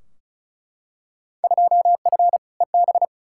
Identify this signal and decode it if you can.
Morse code, heard as a keyed tone, about 36 wpm, 690 Hz. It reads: 2F EB
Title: Field Day Exchanges Encore - 35wpm